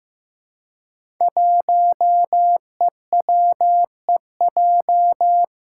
Morse code, heard as a keyed tone, about 15 wpm, 700 Hz. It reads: 1EWEJ
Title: Five Challenging Letter-Number Combinations 15wpm